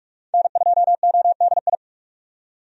Morse code, heard as a keyed tone, about 46 wpm, 700 Hz. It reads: N2ODI
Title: USA Call Signs - 5 Characters - 45wpm